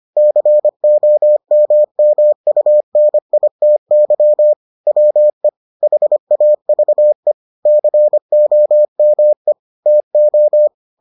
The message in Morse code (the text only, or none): COMMUNITY WE HAVE COME TO